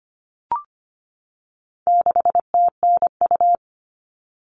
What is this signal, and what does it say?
Morse code, heard as a keyed tone, about 25 wpm, 700 Hz. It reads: 6TNU
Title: All Cut Numbers - 4-Digit 25wpm